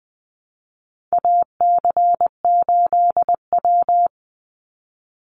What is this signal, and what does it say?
Morse code, heard as a keyed tone, about 20 wpm, 700 Hz. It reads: AC8W
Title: Call Signs Repeated 2x - 20wpm